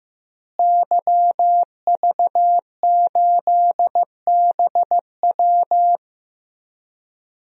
Morse code, heard as a keyed tone, about 15 wpm, 700 Hz. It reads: YV8BW
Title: DXCC-100 Call Signs - 5 Characters - Repeated 2x - 15wpm